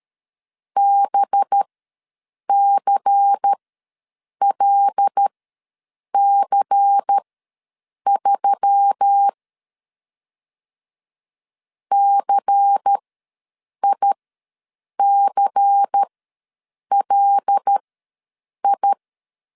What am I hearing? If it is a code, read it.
Morse code, heard as a keyed tone, about 13 wpm, spaced as for 7 wpm, 800 Hz. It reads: BCLC3 CICLI
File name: LEZIONI / 5 LBVC / 5 LBVC ws 12wpm 3159